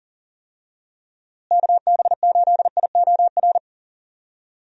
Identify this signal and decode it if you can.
Morse code, heard as a keyed tone, about 37 wpm, 700 Hz. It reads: KB8IOR
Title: USA Call Signs - 6 Characters - Repeated 3x - 40wpm